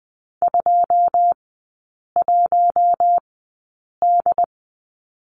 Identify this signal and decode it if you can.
Morse code, heard as a keyed tone, about 20 wpm, spaced as for 8 wpm, 700 Hz. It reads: 21D
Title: 100 Lesson 17e - Letter u - Three Characters 20wpm